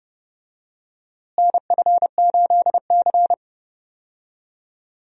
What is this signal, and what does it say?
Morse code, heard as a keyed tone, about 30 wpm, 700 Hz. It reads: NF8C
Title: USA Call Signs - 4 Characters - Repeated 3x - 30wpm